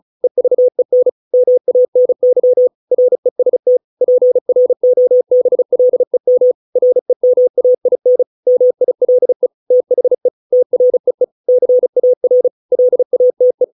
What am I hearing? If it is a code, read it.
Morse code, heard as a keyed tone, about 34 wpm, 500 Hz. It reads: EVEN MANY REST PROBLEM REMAIN MILE THE TREE CAR LATE